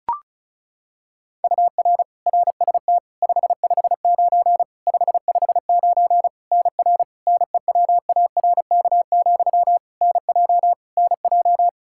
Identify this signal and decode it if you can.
Morse code, heard as a keyed tone, about 36 wpm, 700 Hz. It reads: UR RST 559 559 NR NEWARK, NJ NJ
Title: K1USN SST QSO Practice 35wpm